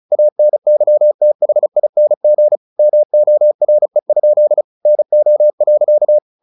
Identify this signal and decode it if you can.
Morse code, heard as a keyed tone, about 34 wpm, 600 Hz. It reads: ANYTHING MORE? NO.
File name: the old man and the sea 2 35wpm